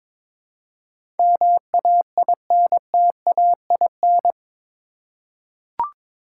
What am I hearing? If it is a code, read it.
Morse code, heard as a keyed tone, about 22 wpm, 700 Hz. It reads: MAINTAIN
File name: practice / 22wpm / Eight Letter Words - 22wpm